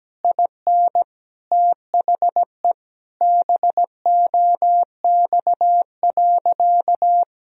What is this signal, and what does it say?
Morse code, heard as a keyed tone, about 17 wpm, 700 Hz. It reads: IN THE BOX.
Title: Sentences from Top 1000 Words - Review Part 2 - 17wpm